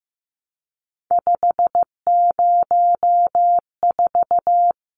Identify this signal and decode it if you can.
Morse code, heard as a keyed tone, about 15 wpm, 700 Hz. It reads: 504
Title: Numbers - 3-Digit 15wpm